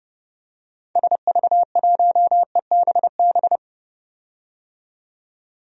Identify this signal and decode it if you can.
Morse code, heard as a keyed tone, about 30 wpm, 700 Hz. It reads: SV1EBB
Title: DXCC-100 Call Signs - 6 Characters - 30wpm